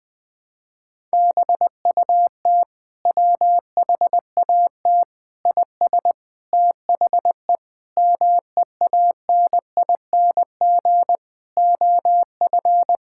Morse code, heard as a keyed tone, about 20 wpm, 700 Hz. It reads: BUT WHAT IS THE MEANING OF
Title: Sentences from Top 1000 Words - Review Part 6 - 20wpm